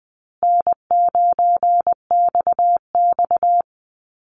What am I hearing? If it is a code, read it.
Morse code, heard as a keyed tone, about 20 wpm, 700 Hz. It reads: N9XX
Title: Call Signs Repeated 2x - 20wpm